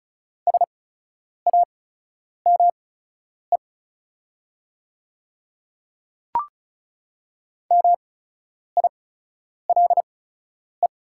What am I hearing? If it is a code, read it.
Morse code, heard as a keyed tone, about 34 wpm, spaced as for 10 wpm, 700 Hz. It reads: SAME MILE
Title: 8x Character Spacing - Top 300 Words - 35wpm